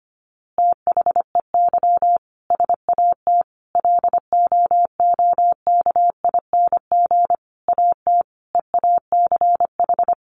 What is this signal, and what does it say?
Morse code, heard as a keyed tone, about 25 wpm, 700 Hz. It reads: THEY SAT LOOKING AT EACH
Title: A Sound Of Thunder 25wpm